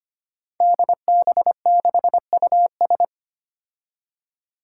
Morse code, heard as a keyed tone, about 25 wpm, 700 Hz. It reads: DB6US
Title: Five Challenging Letter-Number Combinations 25wpm